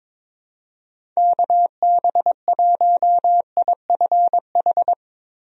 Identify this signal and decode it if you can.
Morse code, heard as a keyed tone, about 21 wpm, 700 Hz. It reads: KB1IFH
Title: USA Call Signs - 6 Characters - Repeated 3x - 22wpm